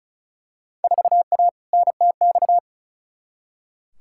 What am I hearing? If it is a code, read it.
Morse code, heard as a keyed tone, about 36 wpm, 700 Hz. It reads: 4A NTX
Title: Field Day Exchanges - 35wpm